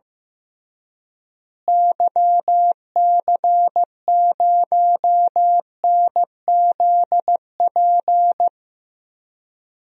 Morse code, heard as a keyed tone, about 15 wpm, 700 Hz. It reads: YC0NZP
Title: DXCC-100 Call Signs - 6 Characters - Repeated 2x - 15wpm